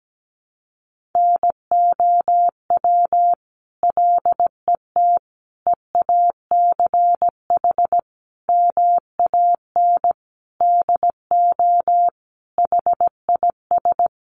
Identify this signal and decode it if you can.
Morse code, heard as a keyed tone, about 17 wpm, 700 Hz. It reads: NOW LET EACH MAN DO HIS
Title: Sentences from Top 1500 Words 17wpm